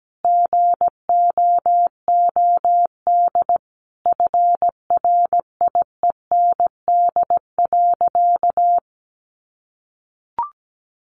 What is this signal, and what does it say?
Morse code, heard as a keyed tone, about 17 wpm, 700 Hz. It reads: GOOD FRIEND.
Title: Sentences from Top 600 Words 17wpm